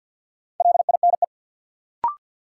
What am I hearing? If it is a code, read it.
Morse code, heard as a keyed tone, about 45 wpm, 700 Hz. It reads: LINE
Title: Sentences from Top 1200 Words - New Word Drill Alt - 50wpm